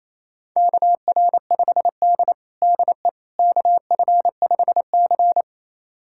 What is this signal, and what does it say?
Morse code, heard as a keyed tone, about 28 wpm, 700 Hz. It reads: KR5D DE KF5C
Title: USA Callsign DE Callsign - 4 Characters - 28wpm